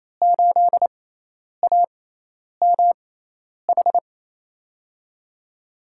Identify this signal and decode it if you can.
Morse code, heard as a keyed tone, about 28 wpm, spaced as for 10 wpm, 700 Hz. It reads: 8AMH
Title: USA Call Signs - 5 Characters - 6x Character Spacing - 28wpm